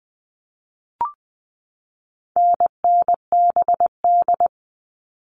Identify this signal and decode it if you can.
Morse code, heard as a keyed tone, about 20 wpm, 700 Hz. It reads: NNBD
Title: All Cut Numbers - 4-Digit 20wpm